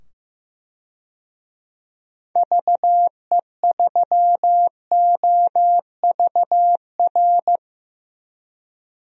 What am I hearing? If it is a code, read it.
Morse code, heard as a keyed tone, about 15 wpm, 700 Hz. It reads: VE3OVR
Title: DXCC-100 Call Signs - 6 Characters - 15wpm